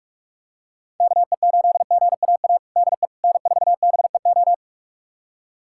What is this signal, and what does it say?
Morse code, heard as a keyed tone, about 46 wpm, 700 Hz. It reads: KE8GAA DE N4BEO
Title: USA Callsign DE Callsign 45wpm